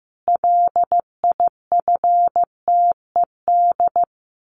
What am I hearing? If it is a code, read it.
Morse code, heard as a keyed tone, about 15 wpm, 700 Hz. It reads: LIFTED
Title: A Sound Of Thunder 15wpm